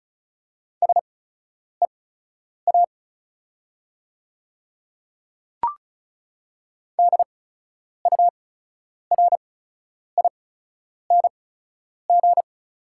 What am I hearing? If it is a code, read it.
Morse code, heard as a keyed tone, about 36 wpm, spaced as for 10 wpm, 700 Hz. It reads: SEA DURING
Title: 8x Character Spacing - Top 500 Words - 35wpm